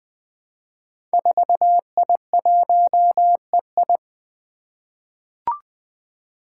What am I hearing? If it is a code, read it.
Morse code, heard as a keyed tone, about 20 wpm, 700 Hz. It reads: 4I1EI
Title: Five Challenging Letter-Number Combinations 20wpm